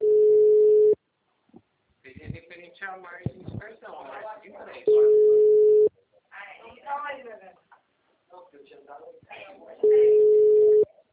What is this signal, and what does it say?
Ringback tone (Europe, 425 Hz), with speech.